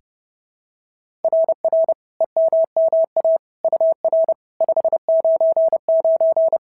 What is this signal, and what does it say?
Morse code, heard as a keyed tone, about 30 wpm, 690 Hz. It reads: RR EMMA UR 599
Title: SOTA QSO Practice 30wpm